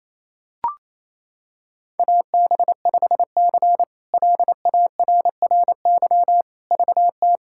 Morse code, heard as a keyed tone, about 28 wpm, 700 Hz. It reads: AB5C LARRY VT
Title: K1USN SST QSO Practice - 28wpm